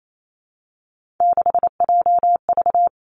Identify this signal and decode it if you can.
Morse code, heard as a keyed tone, about 28 wpm, 700 Hz. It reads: 6JV
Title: Three Challenging Letter-Number Combinations 28wpm